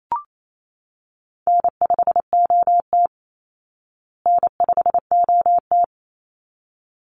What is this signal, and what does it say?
Morse code, heard as a keyed tone, about 28 wpm, 700 Hz. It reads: N5OT N5OT
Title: Call Signs Repeated 3x - 28wpm